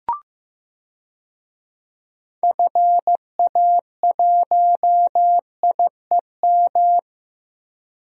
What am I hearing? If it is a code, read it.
Morse code, heard as a keyed tone, about 15 wpm, 700 Hz. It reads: FA1IEM
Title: DXCC-100 Call Signs - 6 Characters - 15wpm